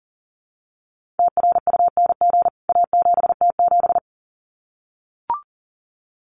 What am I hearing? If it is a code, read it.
Morse code, heard as a keyed tone, about 37 wpm, 700 Hz. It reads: TRUNG A7T7
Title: CWT Exchange 40wpm